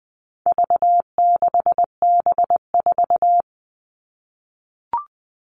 Morse code, heard as a keyed tone, about 20 wpm, 700 Hz. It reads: V6B4
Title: All Cut Numbers - 4-Digit 20wpm